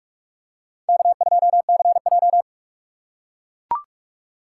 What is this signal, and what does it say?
Morse code, heard as a keyed tone, about 39 wpm, 700 Hz. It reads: K2CJ